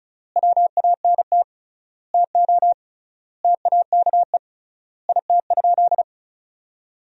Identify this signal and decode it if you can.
Morse code, heard as a keyed tone, about 36 wpm, 700 Hz. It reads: WANT TO TAKE IT?
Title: 3x Word-Spacing - Sentences from Top 300 Words - 35wpm